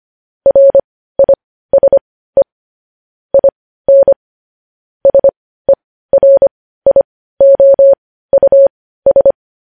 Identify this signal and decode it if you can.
Morse code, heard as a keyed tone, about 25 wpm, spaced as for 15 wpm, 550 Hz. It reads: RISE IN SERIOUS
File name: podcast / News headlines-1x-25x15